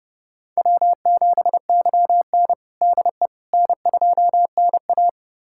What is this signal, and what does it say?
Morse code, heard as a keyed tone, about 30 wpm, 700 Hz. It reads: W7YN DE N2NA
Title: USA Callsign DE Callsign - 4 Characters - 30wpm